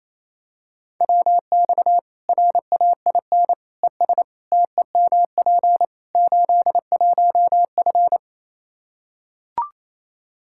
Morse code, heard as a keyed tone, about 28 wpm, 700 Hz. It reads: WX RAIN ES TEMP 81F
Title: Practice QSO - Exchange 2 - Info - 28wpm